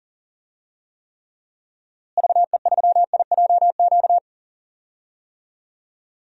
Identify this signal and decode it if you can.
Morse code, heard as a keyed tone, about 38 wpm, 700 Hz. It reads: VE3IJQ
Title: DXCC-100 Call Signs - 6 Characters - Repeated 2x - 40wpm